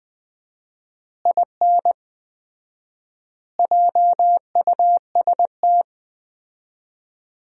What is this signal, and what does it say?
Morse code, heard as a keyed tone, about 20 wpm, 700 Hz. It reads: IN JUST
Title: 4x Word-Spacing - Sentences from Top 300 Words - 20wpm